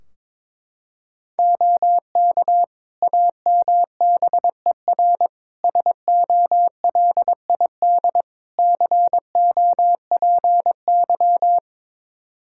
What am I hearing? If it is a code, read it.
Morse code, heard as a keyed tone, about 22 wpm, 700 Hz. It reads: OK AMBER SOLID COPY
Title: Practice QSO - Exchange 3 - More Info - 22wpm